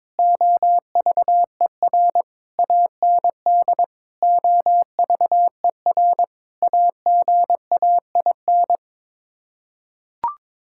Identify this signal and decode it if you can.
Morse code, heard as a keyed tone, about 22 wpm, 700 Hz. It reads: OVER AND OVER AGAIN
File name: practice / 22wpm / Sets of 6 Words - 22wpm